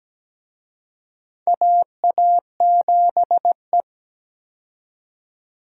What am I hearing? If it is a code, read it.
Morse code, heard as a keyed tone, about 17 wpm, 700 Hz. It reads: AA7E
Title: USA Call Signs - 4 Characters - 17wpm